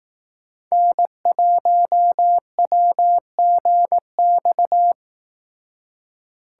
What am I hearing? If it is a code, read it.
Morse code, heard as a keyed tone, about 18 wpm, 700 Hz. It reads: N1WGX
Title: Speed Racing - USA 1x3 Call Signs 15wpm